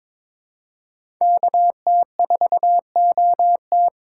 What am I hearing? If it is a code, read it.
Morse code, heard as a keyed tone, about 22 wpm, 700 Hz. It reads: KT4OT
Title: USA Call Signs - 5 Characters - 3x Character Spacing - 22wpm